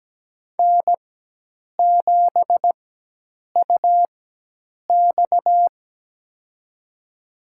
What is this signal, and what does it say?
Morse code, heard as a keyed tone, about 17 wpm, spaced as for 8 wpm, 700 Hz. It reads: N7UX
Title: USA Call Signs - 4 Characters - 4x Character Spacing - 17wpm